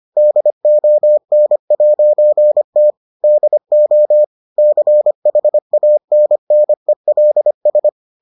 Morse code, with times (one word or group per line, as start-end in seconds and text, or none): 0.2-2.9 DON'T
3.2-4.2 DO
4.6-7.9 CHANNELS